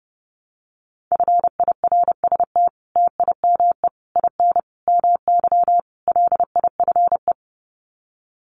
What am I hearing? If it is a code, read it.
Morse code, heard as a keyed tone, about 30 wpm, 700 Hz. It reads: FIRST TIME IN MY LIFE
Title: Sets of 5 Words Encore2 30wpm